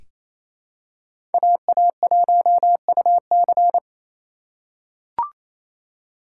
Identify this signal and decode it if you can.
Morse code, heard as a keyed tone, about 28 wpm, 700 Hz. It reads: AA1UC
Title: K1USN SST QSO Practice - 28wpm